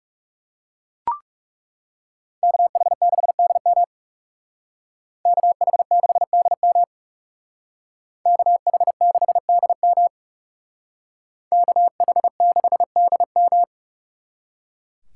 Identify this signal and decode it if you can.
Morse code, heard as a keyed tone, about 37 wpm, 700 Hz. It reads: KH6DM KH6DM KH6DM KH6DM
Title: Speed Racing - USA 2x2 Call Signs 30wpm